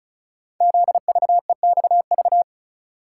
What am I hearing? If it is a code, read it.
Morse code, heard as a keyed tone, about 36 wpm, 690 Hz. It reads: ZVEXV
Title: Code Groups - 5 Letters - 35wpm